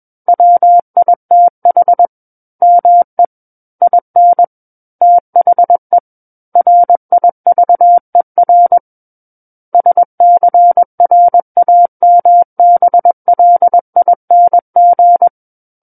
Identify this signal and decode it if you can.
Morse code, heard as a keyed tone, about 21 wpm, 700 Hz. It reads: WITH ME IN THE RIVER SCRAMBLING
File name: Book One / 12-Chapter 12 - 21 WPM